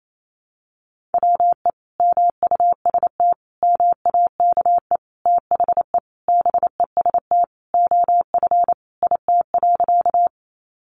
Morse code, heard as a keyed tone, about 28 wpm, 700 Hz. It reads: WE MUST MAKE THE BEST OF IT.
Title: Sentences from Top 400 Words 28wpm